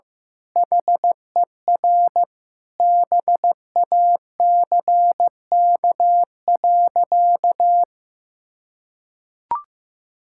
Morse code, heard as a keyed tone, about 15 wpm, 700 Hz. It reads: HER BACK.
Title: Sentences from Top 1000 Words - Review Part 6 - 15wpm